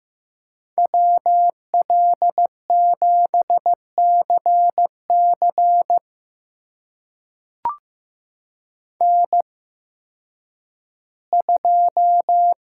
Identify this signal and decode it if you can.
Morse code, heard as a keyed tone, about 15 wpm, spaced as for 10 wpm, 700 Hz. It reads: WL7CC N2
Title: USA Call Signs - 5 Characters - 8x Character Spacing - 15wpm